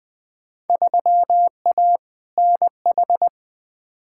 Morse code, heard as a keyed tone, about 21 wpm, 700 Hz. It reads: 3A NH
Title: Field Day Exchanges Encore - 20wpm